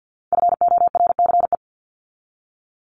Morse code, heard as a keyed tone, about 50 wpm, 700 Hz. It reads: FORCE
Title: ED Suffix - Top 100 Words with Base Words 50wpm